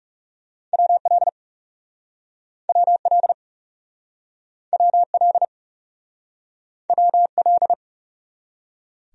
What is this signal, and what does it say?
Morse code, heard as a keyed tone, about 38 wpm, 700 Hz. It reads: WL WL WL WL